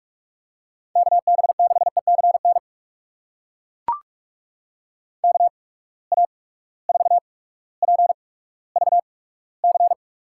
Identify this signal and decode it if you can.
Morse code, heard as a keyed tone, about 41 wpm, spaced as for 19 wpm, 700 Hz. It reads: KB6ECN KA4PVC
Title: USA Call Signs - 6 Characters - 8x Character Spacing - 45wpm